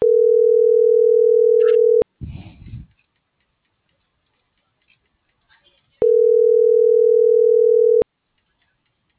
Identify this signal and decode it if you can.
Ringback tone (North America), 440 + 480 Hz.